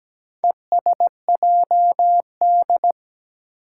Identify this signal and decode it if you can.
Morse code, heard as a keyed tone, about 17 wpm, 700 Hz. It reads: ESJD
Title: Four Challenging Letter-Number Combinations 17wpm